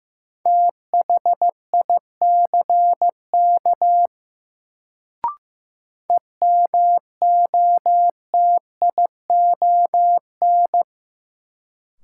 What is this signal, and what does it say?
Morse code, heard as a keyed tone, about 15 wpm, 700 Hz. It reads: THICK EMOTION
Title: Sentences from Top 1600 Words - New Word Drill Alt - 15wpm